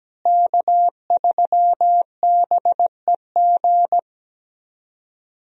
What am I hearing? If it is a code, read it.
Morse code, heard as a keyed tone, about 17 wpm, 700 Hz. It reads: K3BEG
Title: USA Call Signs - 5 Characters - Repeated 2x - 17wpm